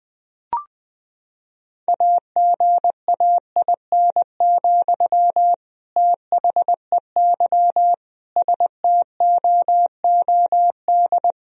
Morse code, heard as a keyed tone, about 20 wpm, 700 Hz. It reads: AGAIN, THEY STOOD
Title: Sentences from Top 1000 Words - Review Part 2 - 20wpm